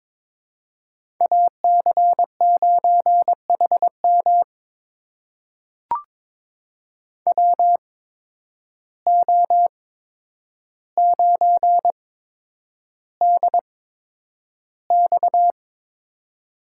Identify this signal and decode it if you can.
Morse code, heard as a keyed tone, about 22 wpm, spaced as for 9 wpm, 700 Hz. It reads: AC9HM WO9DX
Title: USA Call Signs - 5 Characters - 8x Character Spacing - 22wpm